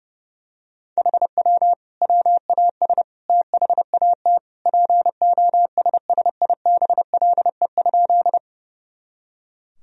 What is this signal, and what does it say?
Morse code, heard as a keyed tone, about 30 wpm, 700 Hz. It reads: HW WAS THAT POSSIBLE?
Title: Sentences from Top 500 Words and Abbreviations 30wpm